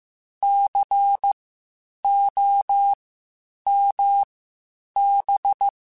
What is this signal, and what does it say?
Morse code, heard as a keyed tone, about 15 wpm, spaced as for 8 wpm, 800 Hz. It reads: COMB
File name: Spaced Alphas / Spaced Alphas mp3s / Cs sp @13